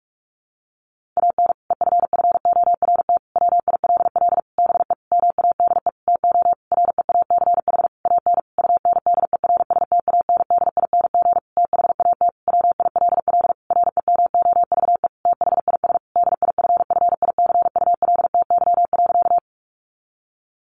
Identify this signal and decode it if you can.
Morse code, heard as a keyed tone, about 44 wpm, 700 Hz. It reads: AN EFFORT WILL BE MADE TO REACH AN UNDERSTANDING THAT WILL REMOVE THIS DIFFICULTY.
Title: Sentences from Top 1500 Words 45wpm